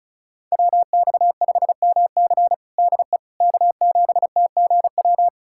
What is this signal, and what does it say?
Morse code, heard as a keyed tone, about 36 wpm, 700 Hz. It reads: WX5MC DE K7TGW
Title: USA Callsign DE Callsign - 5 Characters - 35wpm